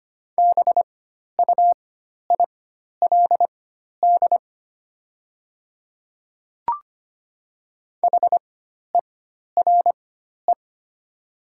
Morse code, heard as a keyed tone, about 25 wpm, spaced as for 12 wpm, 700 Hz. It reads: BUILD HERE
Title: 4x Character Spacing - Top 200 Words - 25wpm